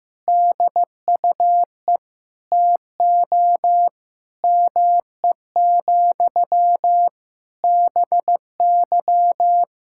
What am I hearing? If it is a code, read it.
Morse code, heard as a keyed tone, about 15 wpm, 700 Hz. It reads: DUE TO ME, BY
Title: Sentences from Top 1000 Words - Review Part 6 - 15wpm